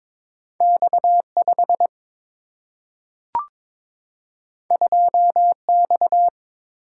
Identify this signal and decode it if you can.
Morse code, heard as a keyed tone, about 22 wpm, 700 Hz. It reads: X 5 2 X
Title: Single-Letter Single-Number 22wpm